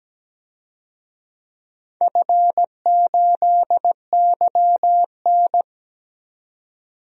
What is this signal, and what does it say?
Morse code, heard as a keyed tone, about 17 wpm, 700 Hz. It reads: F8YN
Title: DXCC-100 Call Signs - 4 Characters - 17wpm